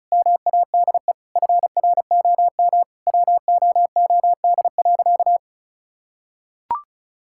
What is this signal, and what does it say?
Morse code, heard as a keyed tone, about 34 wpm, 700 Hz. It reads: MADE FROM WOOD.
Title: Sentences from Top 1000 Words - Review Part 6 - 35wpm